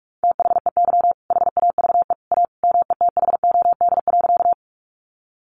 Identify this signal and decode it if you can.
Morse code, heard as a keyed tone, about 44 wpm, 700 Hz. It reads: THEY HAVE A METHOD.
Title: Sentences from Top 600 Words 45wpm